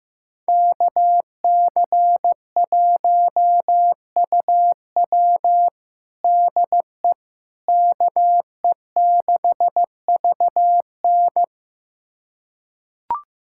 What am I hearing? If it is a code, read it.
Morse code, heard as a keyed tone, about 16 wpm, 700 Hz. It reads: KC1UW DE KE6VN
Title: USA Callsign DE Callsign - 5 Characters - 15wpm